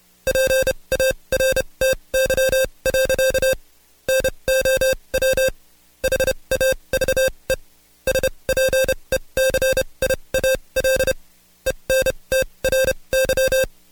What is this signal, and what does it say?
Morse code, heard as a keyed tone, about 29 wpm, spaced as for 23 wpm, 530 Hz. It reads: PARTY. NOW HAVE SPECIAL ENTRY
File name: Code Quick / AudioDownld2 / Seg138-Stepper11 23WPM